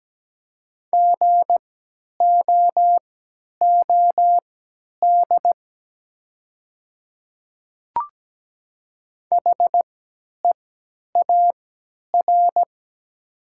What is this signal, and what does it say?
Morse code, heard as a keyed tone, about 17 wpm, spaced as for 10 wpm, 700 Hz. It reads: GOOD HEAR
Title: 3x Character Spacing - Top 300 Words - 17wpm